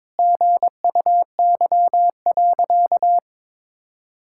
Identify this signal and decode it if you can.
Morse code, heard as a keyed tone, about 21 wpm, 700 Hz. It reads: GUY.
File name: practice / 22wpm / Sentences with 3-Word Phrases - 22wpm